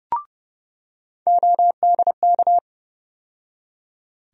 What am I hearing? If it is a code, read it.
Morse code, heard as a keyed tone, about 32 wpm, 700 Hz. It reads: ODK
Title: USA Call Sign Postfixes - 3 Letters 30wpm